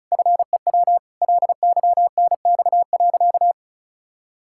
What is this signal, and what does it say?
Morse code, heard as a keyed tone, about 36 wpm, 700 Hz. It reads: FEW LYNX.
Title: Pangram Challenge v2 - 35wpm